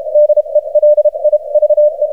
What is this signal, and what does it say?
Morse code, heard as a keyed tone, about 33 wpm, spaced as for 28 wpm, 600 Hz. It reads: DELIVE
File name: archived / CRSnet